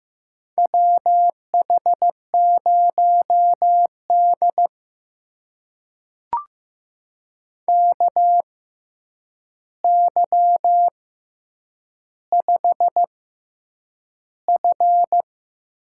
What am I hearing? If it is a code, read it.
Morse code, heard as a keyed tone, about 15 wpm, spaced as for 8 wpm, 700 Hz. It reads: WH0D KY5F
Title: USA Call Signs - 4 Characters - 6x Character Spacing - 15wpm